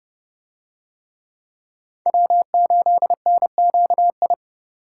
Morse code, heard as a keyed tone, about 29 wpm, 700 Hz. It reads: W8NQI